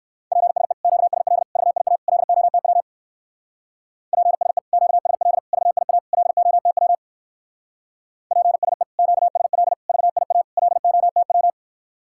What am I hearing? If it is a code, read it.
Morse code, heard as a keyed tone, about 56 wpm, 700 Hz. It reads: PSE QSL VIA LOTW PSE QSL VIA LOTW PSE QSL VIA LOTW
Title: Speed-Racing Common QSO Phrases - 45wpm